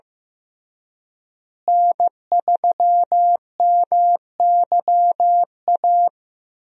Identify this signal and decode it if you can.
Morse code, heard as a keyed tone, about 15 wpm, 700 Hz. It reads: N3MYA
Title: USA Call Signs - 5 Characters - Repeated 2x - 15wpm